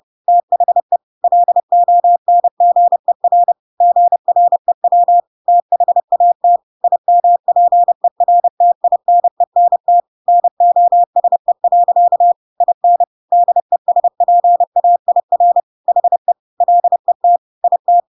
Morse code, heard as a keyed tone, about 30 wpm, 700 Hz. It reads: THE LONGER GREW THAT IMPERTINENT NOSE. IN DESPAIR HE LET IT
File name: Pinocchio / Pinocchio ch3 30 30 ik1hge